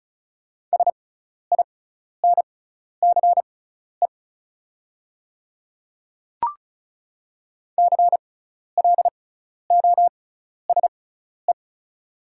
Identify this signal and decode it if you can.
Morse code, heard as a keyed tone, about 36 wpm, spaced as for 12 wpm, 700 Hz. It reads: SINCE CLOSE
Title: 6x Character Spacing - Top 300 Words - 35wpm